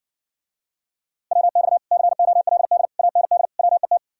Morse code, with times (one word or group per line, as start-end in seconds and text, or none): 1.3-1.8 WX
1.9-2.9 COLD
3.0-3.5 AND
3.6-4.0 WET